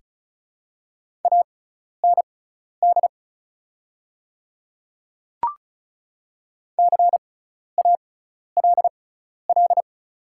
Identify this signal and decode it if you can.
Morse code, heard as a keyed tone, about 36 wpm, spaced as for 12 wpm, 690 Hz. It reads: AND CALL
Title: 6x Character Spacing - Top 100 Words - 35wpm